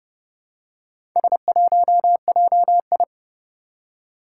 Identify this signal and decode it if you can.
Morse code, heard as a keyed tone, about 30 wpm, 700 Hz. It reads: S1JI
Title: Four Challenging Letter-Number Combinations 30wpm